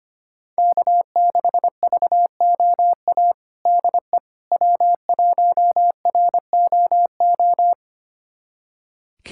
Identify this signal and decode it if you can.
Morse code, heard as a keyed tone, about 25 wpm, 700 Hz. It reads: K6VOA DE W1ROO
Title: USA Callsign DE Callsign - 5 Characters - 25wpm